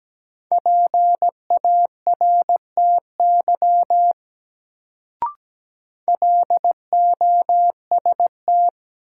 Morse code, heard as a keyed tone, about 17 wpm, 700 Hz. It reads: PARTY LOST
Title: Sentences from Top 1400 Words - New Word Drill Alt - 17wpm